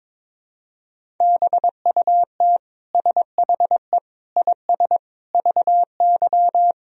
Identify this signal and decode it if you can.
Morse code, heard as a keyed tone, about 22 wpm, 700 Hz. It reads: BUT SHE IS VY
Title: Sentences from Top 400 Words and Abbreviations 22wpm